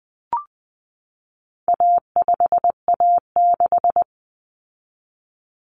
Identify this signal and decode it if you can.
Morse code, heard as a keyed tone, about 20 wpm, 700 Hz. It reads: A5A6
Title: Common Cut Numbers - 4-Digit 20wpm